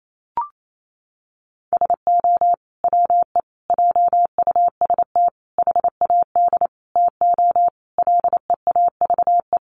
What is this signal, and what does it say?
Morse code, heard as a keyed tone, about 28 wpm, 700 Hz. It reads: SO WE JUST HAD TO LEAVE
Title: Sentences from Top 1500 Words 28wpm